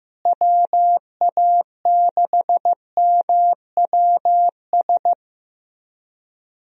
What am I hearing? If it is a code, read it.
Morse code, heard as a keyed tone, about 15 wpm, 700 Hz. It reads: WA6MWS
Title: USA Call Signs - 6 Characters - Repeated 3x - 15wpm